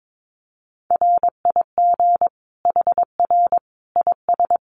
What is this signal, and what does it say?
Morse code, heard as a keyed tone, about 22 wpm, 700 Hz. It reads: RIG HR IS
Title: Common QSO Phrases 22wpm